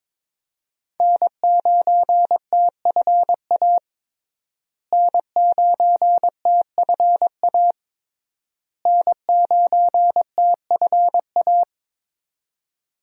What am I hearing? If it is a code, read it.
Morse code, heard as a keyed tone, about 22 wpm, 700 Hz. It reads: N9TFA N9TFA N9TFA
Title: USA Call Signs - 5 Characters - Repeated 3x - 22wpm